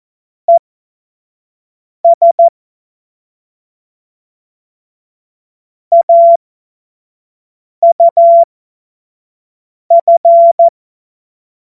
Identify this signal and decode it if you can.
Morse code, heard as a keyed tone, about 14 wpm, spaced as for 5 wpm, 680 Hz. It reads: ES AUF
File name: Morsekurs / Witze / Witz 23 April 2024 25 F70